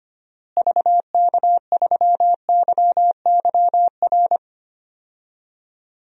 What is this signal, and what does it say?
Morse code, heard as a keyed tone, about 25 wpm, 700 Hz. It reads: VK3YYR
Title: DXCC-100 Call Signs - 6 Characters - Repeated 2x - 25wpm